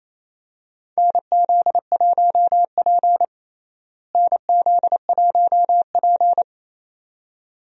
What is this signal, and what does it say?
Morse code, heard as a keyed tone, about 28 wpm, 700 Hz. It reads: NZ1P NZ1P
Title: USA Call Signs - 4 Characters - Repeated 2x - 28wpm